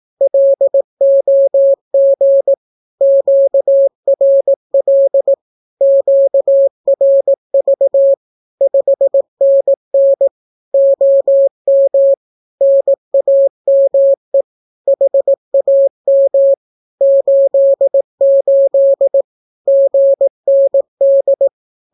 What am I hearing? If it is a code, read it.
Morse code, heard as a keyed tone, about 18 wpm, 550 Hz. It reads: LOG QRL QRV 5NN OM NAME HAM 88 GND
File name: naukapodstaw / skroty / skroty